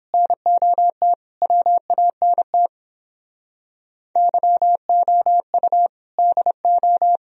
Morse code, heard as a keyed tone, about 29 wpm, 700 Hz. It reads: NOT WANT YOU DO